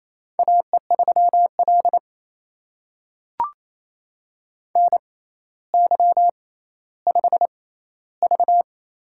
Morse code, heard as a keyed tone, about 28 wpm, spaced as for 14 wpm, 700 Hz. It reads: AE3L NY5V
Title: USA Call Signs - 4 Characters - 6x Character Spacing - 28wpm